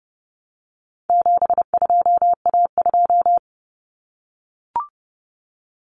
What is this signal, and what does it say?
Morse code, heard as a keyed tone, about 30 wpm, 700 Hz. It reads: 72A2
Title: Common Cut Numbers - 4-Digit 30wpm